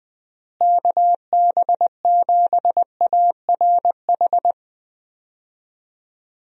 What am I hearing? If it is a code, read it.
Morse code, heard as a keyed tone, about 21 wpm, 700 Hz. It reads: KB7ARH
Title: USA Call Signs - 6 Characters - Repeated 3x - 20wpm